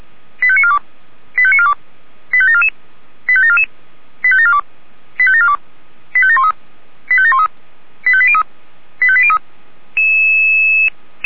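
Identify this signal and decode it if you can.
Five-tone selective call (ZVEI), heard as quick runs of five tones, tones of about 70 ms. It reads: 85942 85942 85744 85744 85742 85742 85713 85713 85803 85803
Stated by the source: ZVEI five-tone call received by SDR.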